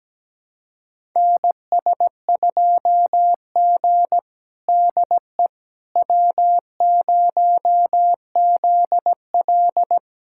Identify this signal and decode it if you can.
Morse code, heard as a keyed tone, about 17 wpm, 700 Hz. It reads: NS2G DE W0ZL
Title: USA Callsign DE Callsign - 4 Characters - 17wpm